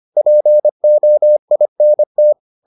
Morse code, heard as a keyed tone, about 25 wpm, 600 Hz. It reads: POINT